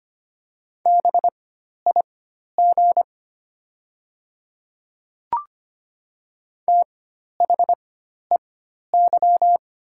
Morse code, heard as a keyed tone, about 25 wpm, spaced as for 12 wpm, 700 Hz. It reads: BIG THEY
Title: 4x Character Spacing - Top 200 Words - 25wpm